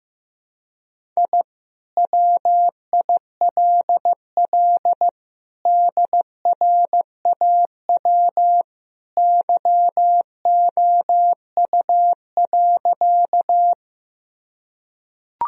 Morse code, heard as a keyed tone, about 15 wpm, 700 Hz. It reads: I WILL DRAW YOU.
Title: Sentences from Top 1000 Words - Review Part 1 - 15wpm